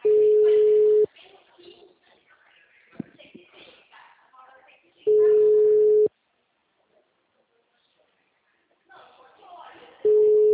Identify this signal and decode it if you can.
Ringback tone (Europe, 425 Hz), with speech.
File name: out- -2006- - - .3252